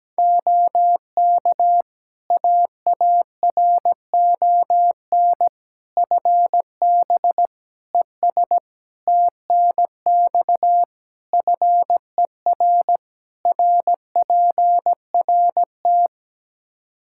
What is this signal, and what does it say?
Morse code, heard as a keyed tone, about 17 wpm, 700 Hz. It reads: OK AARON FB ES TNX FER RPRT
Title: K1USN SST QSO Practice 17wpm